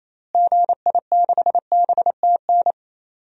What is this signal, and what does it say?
Morse code, heard as a keyed tone, about 28 wpm, 700 Hz. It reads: GI6BTN
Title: DXCC-100 Call Signs - 6 Characters - Repeated 3x - 28wpm